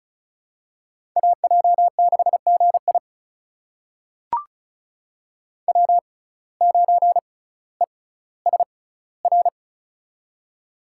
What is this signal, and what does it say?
Morse code, heard as a keyed tone, about 36 wpm, spaced as for 18 wpm, 700 Hz. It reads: AJ6GI W9ESR
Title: USA Call Signs - 5 Characters - 6x Character Spacing - 35wpm